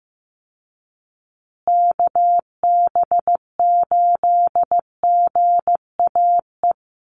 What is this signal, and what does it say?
Morse code, heard as a keyed tone, about 15 wpm, 700 Hz. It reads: KB8GAE
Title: Call Signs Repeated 3x - 15wpm